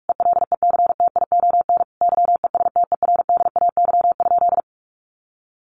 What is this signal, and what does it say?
Morse code, heard as a keyed tone, about 44 wpm, 700 Hz. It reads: ELECTION YESTERDAY?
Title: A Sound Of Thunder 45wpm